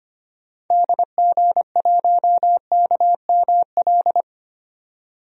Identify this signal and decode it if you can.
Morse code, heard as a keyed tone, about 25 wpm, 700 Hz. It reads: DG1KML
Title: DXCC-100 Call Signs - 6 Characters - 25wpm